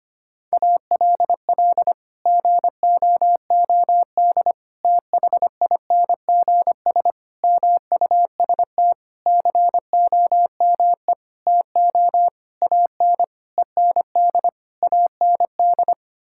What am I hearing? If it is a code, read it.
Morse code, heard as a keyed tone, about 25 wpm, 700 Hz. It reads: ALL GOOD THINGS MUST COME TO AN END AND
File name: Sets of 9 Words - 25wpm